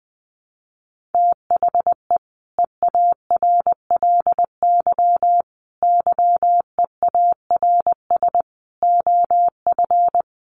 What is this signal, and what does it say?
Morse code, heard as a keyed tone, about 20 wpm, 700 Hz. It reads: THE EARLY YEARS OF
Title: Sets of 5 Words Encore 20wpm